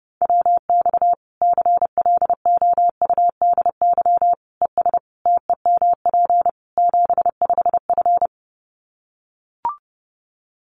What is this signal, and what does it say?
Morse code, heard as a keyed tone, about 30 wpm, 700 Hz. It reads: WX CLOUDY ES TEMP 75F
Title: Common QSO Phrases 30wpm